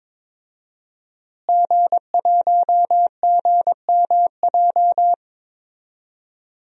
Morse code, heard as a keyed tone, about 21 wpm, 700 Hz. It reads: G1GMJ
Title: DXCC-100 Call Signs - 5 Characters - Repeated 2x - 22wpm